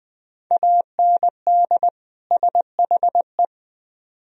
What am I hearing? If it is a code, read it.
Morse code, heard as a keyed tone, about 20 wpm, 700 Hz. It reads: AND SHE